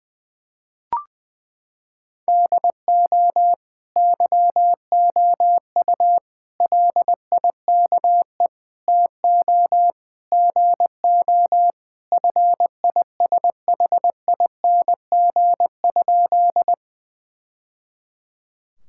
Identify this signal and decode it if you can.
Morse code, heard as a keyed tone, about 21 wpm, 700 Hz. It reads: DO YOU LIKE TO GO FISHING?
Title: Sentences from Top 1000 Words - Review Part 5 - 20wpm